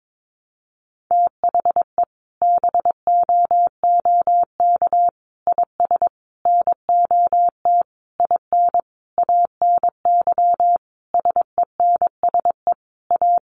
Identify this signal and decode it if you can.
Morse code, heard as a keyed tone, about 22 wpm, 700 Hz. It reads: THE BOOK IS NOT IN ANY SENSE A
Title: Sentences from Top 1300 Words 22wpm